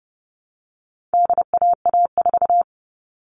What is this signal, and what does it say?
Morse code, heard as a keyed tone, about 30 wpm, 700 Hz. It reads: DAA4
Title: All Cut Numbers - 4-Digit 30wpm